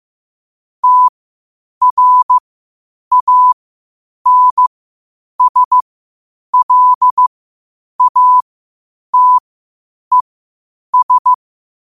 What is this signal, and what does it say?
Morse code, heard as a keyed tone, about 15 wpm, spaced as for 8 wpm, 1000 Hz. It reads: TRANSLATES
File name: qotdfiles / 8wpm